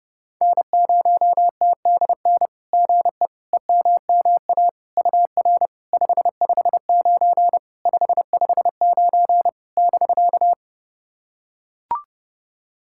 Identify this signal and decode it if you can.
Morse code, heard as a keyed tone, about 31 wpm, 690 Hz. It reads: N0TDN GE EMMA UR 559 559 <BK>
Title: SOTA QSO Practice 30wpm